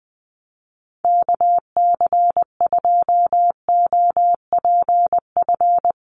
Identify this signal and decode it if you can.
Morse code, heard as a keyed tone, about 20 wpm, 700 Hz. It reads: KC2OPF
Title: USA Call Signs Repeated 2x 20wpm